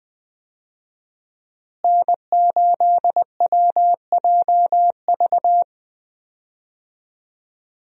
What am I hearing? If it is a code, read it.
Morse code, heard as a keyed tone, about 20 wpm, 700 Hz. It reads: N8WJV
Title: USA Call Signs - 5 Characters - Repeated 3x - 20wpm